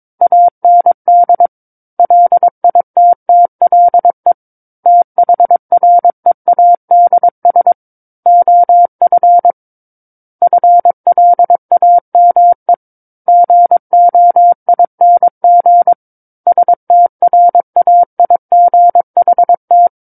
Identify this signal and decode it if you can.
Morse code, heard as a keyed tone, about 22 wpm, 700 Hz. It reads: AND LITTLE THREADS OF FLAME GOING STRAIGHT
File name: Book One / 13-Chapter 13 - 22 WPM